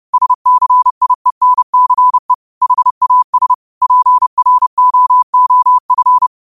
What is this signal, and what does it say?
Morse code, heard as a keyed tone, about 30 wpm, 1000 Hz. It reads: SCIENCE HAS PROOF
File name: qotdfiles / 30wpm